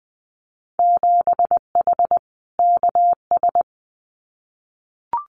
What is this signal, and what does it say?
Morse code, heard as a keyed tone, about 20 wpm, 700 Hz. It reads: 7H KS
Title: Winter Field Day Exchanges Encore 20wpm